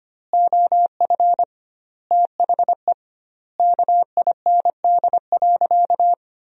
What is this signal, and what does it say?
Morse code, heard as a keyed tone, about 25 wpm, 700 Hz. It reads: OF THE KIND.
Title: 2x Word-Spacing - Sentences from Top 500 Words - 25wpm